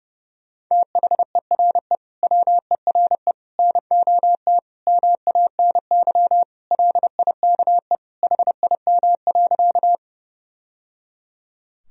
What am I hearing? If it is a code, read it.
Morse code, heard as a keyed tone, about 32 wpm, 700 Hz. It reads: THERE WERE NOT MANY LIKE HIM.
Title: Sentences from Top 1000 Words - Review Part 1 - 30wpm